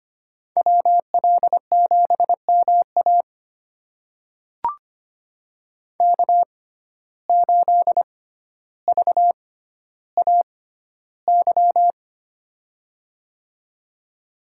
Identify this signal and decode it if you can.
Morse code, heard as a keyed tone, about 25 wpm, spaced as for 13 wpm, 700 Hz. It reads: WL7MA K8VAY
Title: USA Call Signs - 5 Characters - 6x Character Spacing - 25wpm